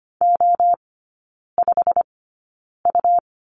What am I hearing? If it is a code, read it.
Morse code, heard as a keyed tone, about 25 wpm, spaced as for 9 wpm, 700 Hz. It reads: O5U
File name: Lesson 17e - Letter u - Three Characters 25wpm